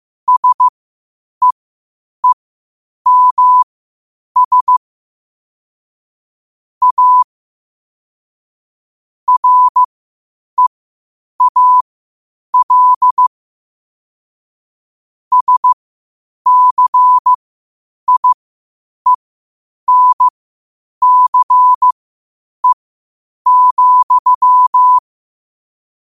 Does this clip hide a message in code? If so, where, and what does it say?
Yes: Morse code from 0.3 to 25.0 s, heard as a keyed tone, about 15 wpm, spaced as for 8 wpm, 1000 Hz. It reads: SEEMS A REAL SCIENCE,